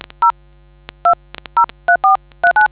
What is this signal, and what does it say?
DTMF, heard as telephone touch tones, tones of about 110 ms. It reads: 0 203439